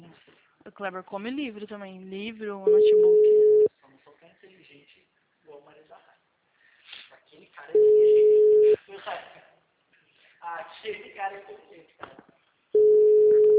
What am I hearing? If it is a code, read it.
Ringback tone (Europe, 425 Hz), with speech.
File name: out- -2027- - - .372